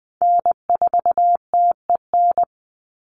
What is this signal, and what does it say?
Morse code, heard as a keyed tone, about 20 wpm, 700 Hz. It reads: N4TEN
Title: USA Call Signs 20wpm